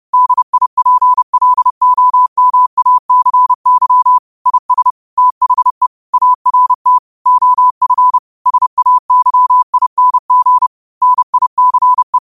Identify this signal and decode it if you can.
Morse code, heard as a keyed tone, about 30 wpm, 1000 Hz. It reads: DIPLOMACY IS THE ART OF SAYING NICE